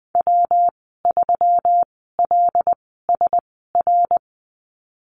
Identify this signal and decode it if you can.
Morse code, heard as a keyed tone, about 20 wpm, spaced as for 14 wpm, 700 Hz. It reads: W3LSR
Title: 193 Lesson 31g - Letter v - Call Signs 20wpm